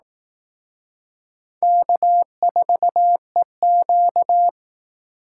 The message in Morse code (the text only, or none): K4EQ